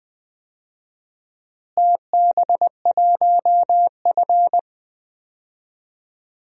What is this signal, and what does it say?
Morse code, heard as a keyed tone, about 20 wpm, 700 Hz. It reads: TB1F
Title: DXCC-100 Call Signs - 4 Characters - Repeated 2x - 20wpm